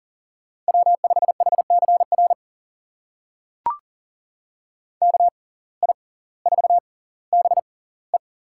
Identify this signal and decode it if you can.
Morse code, heard as a keyed tone, about 40 wpm, spaced as for 20 wpm, 700 Hz. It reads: W5HCR KI4BE
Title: USA Call Signs - 6x Character Spacing - 40wpm